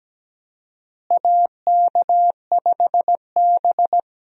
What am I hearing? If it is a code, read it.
Morse code, heard as a keyed tone, about 17 wpm, 700 Hz. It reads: AK5B
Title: USA Call Signs - 4 Characters - Repeated 2x - 17wpm